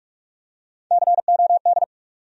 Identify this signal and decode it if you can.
Morse code, heard as a keyed tone, about 47 wpm, 700 Hz. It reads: COD
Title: USA Call Sign Postfixes - 3 Letters 45wpm